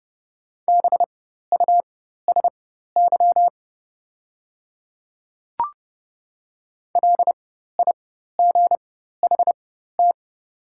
Morse code, heard as a keyed tone, about 30 wpm, spaced as for 14 wpm, 700 Hz. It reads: BUSY LIGHT
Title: 4x Character Spacing - Top 500 Words - 30wpm